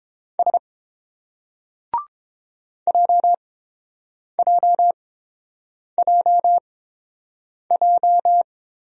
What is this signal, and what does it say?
Morse code, heard as a keyed tone, about 31 wpm, spaced as for 7 wpm, 700 Hz. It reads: S JJJJ